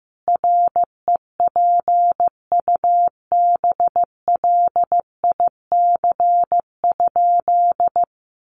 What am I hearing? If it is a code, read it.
Morse code, heard as a keyed tone, about 14 wpm, 700 Hz. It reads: REPUBLIC?
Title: Sentences from Top 1500 Words 15wpm